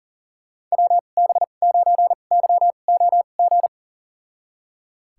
Morse code, heard as a keyed tone, about 40 wpm, spaced as for 29 wpm, 700 Hz. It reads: WB9YOG
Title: USA Call Signs - 6 Characters - 2x Character Spacing - 40wpm